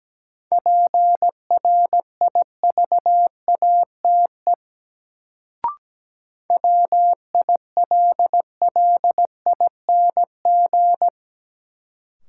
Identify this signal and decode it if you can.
Morse code, heard as a keyed tone, about 17 wpm, 700 Hz. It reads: PRIVATE WILLING
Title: Sentences from Top 1000 Words - New Word Drill - 17wpm